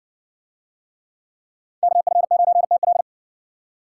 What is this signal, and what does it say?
Morse code, heard as a keyed tone, about 56 wpm, 700 Hz. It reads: KF9TL